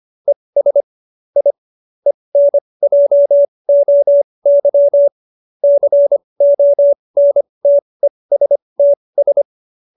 Morse code, heard as a keyed tone, about 25 wpm, spaced as for 20 wpm, 570 Hz. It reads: ES I ENJOY CONTESTS